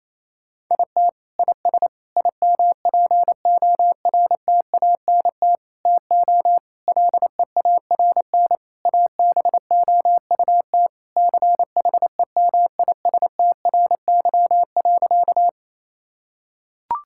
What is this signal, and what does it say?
Morse code, heard as a keyed tone, about 28 wpm, 700 Hz. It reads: IT IS IMPORTANT TO LEARN ABOUT CHEMISTRY.
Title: Sentences from Top 1000 Words - Review Part 6 - 28wpm